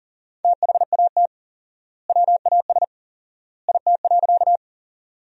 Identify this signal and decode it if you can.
Morse code, heard as a keyed tone, about 40 wpm, 700 Hz. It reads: THAT WAS IT.
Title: 4x Word-Spacing - Sentences from Top 100 Words - 40wpm